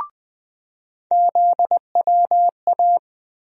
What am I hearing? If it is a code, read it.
Morse code, heard as a keyed tone, about 20 wpm, 700 Hz. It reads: ZWA
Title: USA Call Sign Postfixes - 3 Letters 20wpm